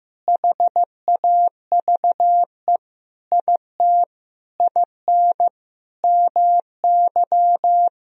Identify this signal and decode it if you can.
Morse code, heard as a keyed tone, about 15 wpm, 700 Hz. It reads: HAVE IT IN MY
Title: Sentences from Top 1000 Words - Review Part 1 - 15wpm